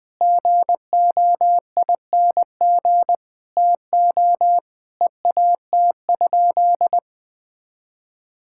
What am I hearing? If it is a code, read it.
Morse code, heard as a keyed tone, about 20 wpm, 700 Hz. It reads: GOING TO EAT?
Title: Sentences with 5-Word Phrases - 20wpm